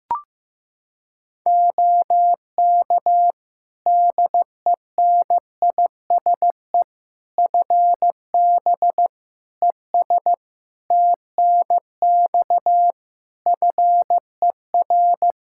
Morse code, heard as a keyed tone, about 15 wpm, 700 Hz. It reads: OK DENISE FB ES TNX FER
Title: Practice QSO - Exchange 2 - Info - 15wpm